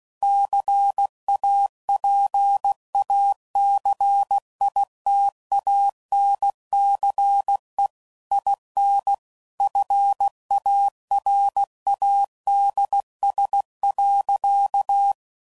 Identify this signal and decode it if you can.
Morse code, heard as a keyed tone, about 16 wpm, 800 Hz. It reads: CAPACITANCE IN FARADS.
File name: CW Code Groups @15WPM / Technical Subjects / XC